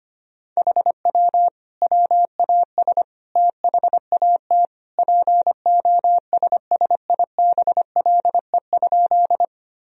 Morse code, heard as a keyed tone, about 25 wpm, 700 Hz. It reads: HW WAS THAT POSSIBLE?
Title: Sentences from Top 500 Words and Abbreviations 25wpm